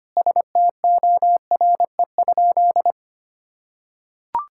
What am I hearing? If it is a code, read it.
Morse code, heard as a keyed tone, about 25 wpm, 700 Hz. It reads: STORE?
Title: Sentences from Top 1000 Words - Review Part 5 - 25wpm